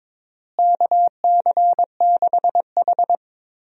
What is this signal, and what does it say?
Morse code, heard as a keyed tone, about 22 wpm, 700 Hz. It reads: KC6H
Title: USA Call Signs - 4 Characters - Repeated 2x - 22wpm